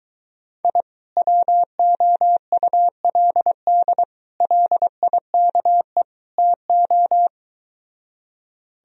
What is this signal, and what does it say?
Morse code, heard as a keyed tone, about 23 wpm, 700 Hz. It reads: I WOULD LIKE TO